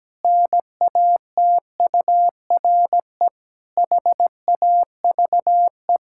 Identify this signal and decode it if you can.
Morse code, heard as a keyed tone, about 17 wpm, 700 Hz. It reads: NATURE HAVE
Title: Pangram Challenge v2 - 17wpm